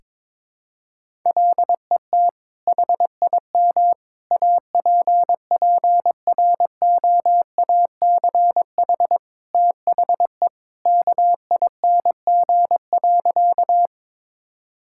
Morse code, heard as a keyed tone, about 22 wpm, 700 Hz. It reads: LET HIM APPROACH THE KING.
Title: Sentences from Top 1000 Words - Review Part 6 - 22wpm